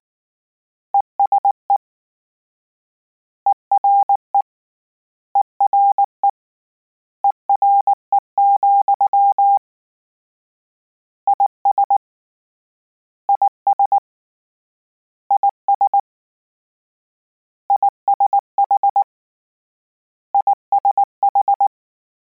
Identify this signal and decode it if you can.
Morse code, heard as a keyed tone, about 19 wpm, 800 Hz. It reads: ESE ERE ERE ERE, IS IS IS ISH ISH